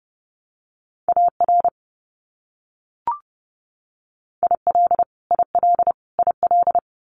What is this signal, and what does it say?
Morse code, heard as a keyed tone, about 30 wpm, 700 Hz. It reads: AR IL IL IL
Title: US States Repeated 3x 30wpm